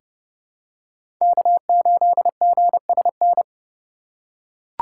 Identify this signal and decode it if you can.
Morse code, heard as a keyed tone, about 30 wpm, 700 Hz. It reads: K8GSN
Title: USA Call Signs - 5 Characters - 6x Character Spacing - 30wpm